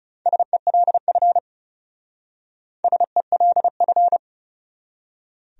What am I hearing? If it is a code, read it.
Morse code, heard as a keyed tone, about 36 wpm, 700 Hz. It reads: SELF SELF
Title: Speed-Racing Top 200 Words 30wpm